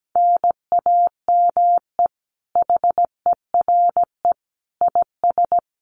Morse code, heard as a keyed tone, about 17 wpm, 700 Hz. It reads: NAME HERE IS
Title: Common QSO Phrases 17wpm